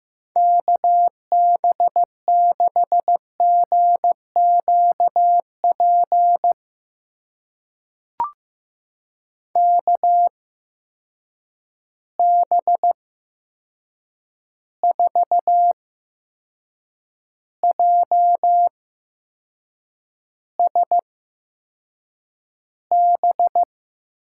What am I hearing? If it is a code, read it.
Morse code, heard as a keyed tone, about 15 wpm, spaced as for 6 wpm, 700 Hz. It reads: KB6GQP KB4JSB
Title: USA Call Signs - 6 Characters - 8x Character Spacing - 15wpm